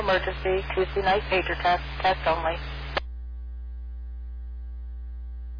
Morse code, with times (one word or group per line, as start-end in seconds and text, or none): none